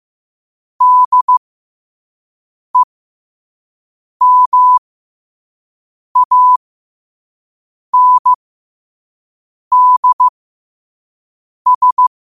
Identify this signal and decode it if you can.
Morse code, heard as a keyed tone, about 15 wpm, spaced as for 5 wpm, 1000 Hz. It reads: DEMANDS
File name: qotdfiles / 5wpm